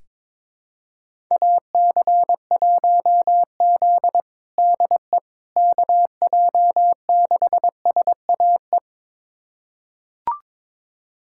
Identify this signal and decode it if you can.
Morse code, heard as a keyed tone, about 22 wpm, 700 Hz. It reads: AC1Z DE KJ6SAE
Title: USA Callsign DE Callsign 22wpm